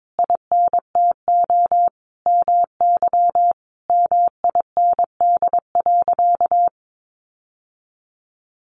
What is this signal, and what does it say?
Morse code, heard as a keyed tone, about 22 wpm, 700 Hz. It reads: INTO MY MIND.
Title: Sentences from Top 1300 Words 22wpm